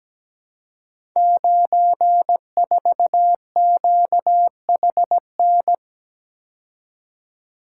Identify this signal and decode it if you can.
Morse code, heard as a keyed tone, about 17 wpm, 700 Hz. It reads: 94QHN
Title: Code Groups - 5 Letter:Numbers - 17wpm